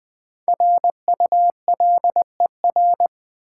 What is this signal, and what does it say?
Morse code, heard as a keyed tone, about 20 wpm, 700 Hz. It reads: RULER
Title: Easy Words with Hints - 20wpm